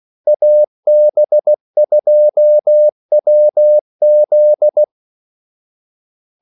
Morse code, heard as a keyed tone, about 16 wpm, 600 Hz. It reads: AB2WZ
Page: Kaldesignaler 16 wpm